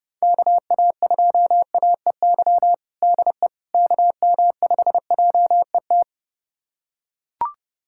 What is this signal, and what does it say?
Morse code, heard as a keyed tone, about 30 wpm, 700 Hz. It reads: KA2AEY DE KM5JET
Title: USA Callsign DE Callsign - 6 Characters - 30wpm